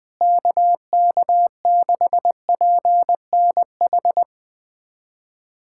Morse code, heard as a keyed tone, about 21 wpm, 700 Hz. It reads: KK6PNH
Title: USA Callsign DE Callsign - 6 Characters - 20wpm